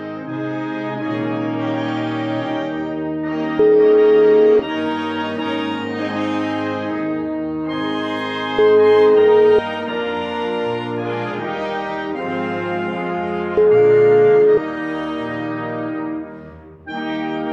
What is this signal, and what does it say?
Ringback tone (Europe, 425 Hz), with noise.